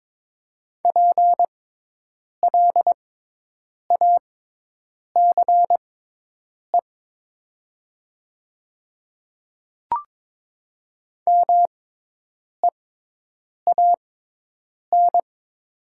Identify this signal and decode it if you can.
Morse code, heard as a keyed tone, about 22 wpm, spaced as for 8 wpm, 700 Hz. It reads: PLACE MEAN
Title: 6x Character Spacing - Top 300 Words - 22wpm